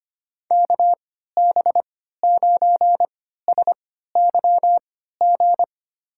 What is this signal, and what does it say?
Morse code, heard as a keyed tone, about 25 wpm, spaced as for 14 wpm, 700 Hz. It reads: KB9SYG
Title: USA Call Signs - 6 Characters - 3x Character Spacing - 25wpm